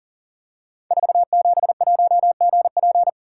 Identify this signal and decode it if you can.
Morse code, heard as a keyed tone, about 39 wpm, 700 Hz. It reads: 471GP
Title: Code Groups - 5 Letter:Numbers - 40wpm